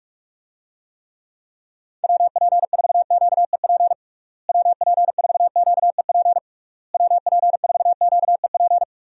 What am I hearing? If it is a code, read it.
Morse code, heard as a keyed tone, about 47 wpm, 700 Hz. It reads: WP4QEP WP4QEP WP4QEP
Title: USA Call Signs - 6 Characters - Repeated 3x - 45wpm